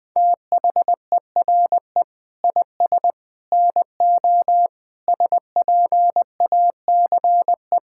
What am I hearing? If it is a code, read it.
Morse code, heard as a keyed tone, about 20 wpm, 700 Hz. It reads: THERE IS NO SPACE
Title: Sentences from Top 1000 Words - Review Part 5 - 20wpm